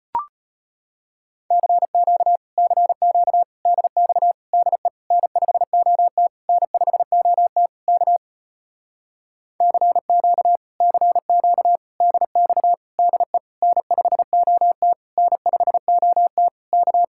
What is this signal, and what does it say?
Morse code, heard as a keyed tone, about 38 wpm, 700 Hz. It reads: CQ CQ DX DE N5OT N5OT K CQ CQ DX DE N5OT N5OT K